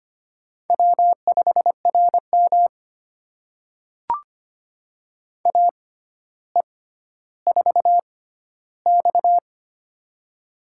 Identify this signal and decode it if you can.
Morse code, heard as a keyed tone, about 25 wpm, spaced as for 13 wpm, 700 Hz. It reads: W5RM AE4X
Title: USA Call Signs - 4 Characters - 6x Character Spacing - 25wpm